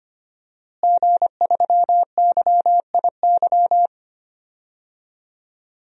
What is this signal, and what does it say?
Morse code, heard as a keyed tone, about 25 wpm, 700 Hz. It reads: G3YIY
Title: DXCC-100 Call Signs - 5 Characters - Repeated 2x - 25wpm